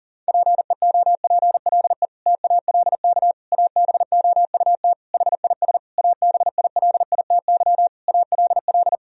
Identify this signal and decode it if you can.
Morse code, heard as a keyed tone, about 40 wpm, 700 Hz. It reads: PEOPLE TALK ABOUT HIS ABILITY ALL
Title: Sentences from Top 1000 Words - Review Part 3 - 40wpm